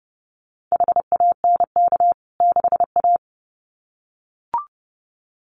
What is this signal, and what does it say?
Morse code, heard as a keyed tone, about 30 wpm, 700 Hz. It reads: HANK 6A
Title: CWT Exchange 30wpm